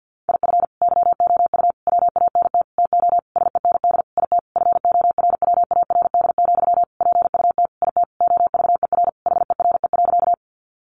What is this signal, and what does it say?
Morse code, heard as a keyed tone, about 50 wpm, 700 Hz. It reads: IF YOU WANT TO SEND IT FORWARD, PUT IT OVER HERE.
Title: Sentences from Top 1500 Words 50wpm